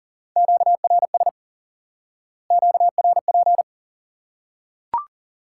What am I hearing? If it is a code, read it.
Morse code, heard as a keyed tone, about 40 wpm, 700 Hz. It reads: QRS QRP
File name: IQE Course 3 - Lesson 89 - YAGI - Two Elements - 40wpm